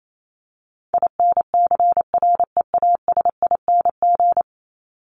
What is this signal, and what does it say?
Morse code, heard as a keyed tone, about 28 wpm, 700 Hz. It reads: INCREASING
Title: ING Suffix - Top 100 Words with Base Words 28wpm